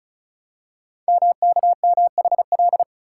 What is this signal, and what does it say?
Morse code, heard as a keyed tone, about 36 wpm, 700 Hz. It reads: MKMHL
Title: Code Groups - 5 Letters - 35wpm